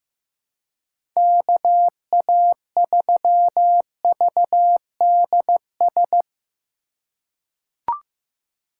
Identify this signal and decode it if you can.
Morse code, heard as a keyed tone, about 15 wpm, 700 Hz. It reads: KA3VDS
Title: USA Call Signs - 6 Characters - 2x Character Spacing - 15wpm